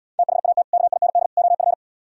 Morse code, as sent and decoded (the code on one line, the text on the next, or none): - .... .- - / -.- .. -. -.. / --- ..-.
THAT KIND OF